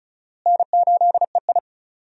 Morse code, heard as a keyed tone, about 36 wpm, 700 Hz. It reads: N8EI
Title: USA Call Signs - 4 Characters - Repeated 2x - 35wpm